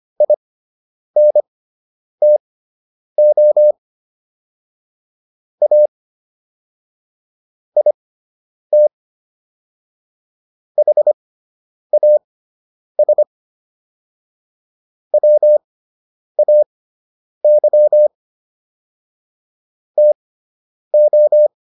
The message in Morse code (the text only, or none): INTO A IT HAS WAY TO